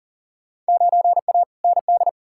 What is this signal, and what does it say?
Morse code, heard as a keyed tone, about 40 wpm, 700 Hz. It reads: 9A ND
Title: Field Day Exchanges Encore - 40wpm